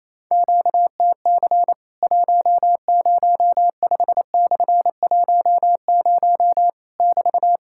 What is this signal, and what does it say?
Morse code, heard as a keyed tone, about 28 wpm, 700 Hz. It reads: QTC 105/10 <BT>
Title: Worked All Europe DX Contest - QTC Exchange - 28wpm